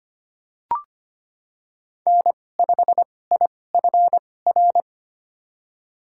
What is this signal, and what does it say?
Morse code, heard as a keyed tone, about 25 wpm, spaced as for 18 wpm, 700 Hz. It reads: N5IFR
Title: USA Call Signs - 2x Character Spacing - 25wpm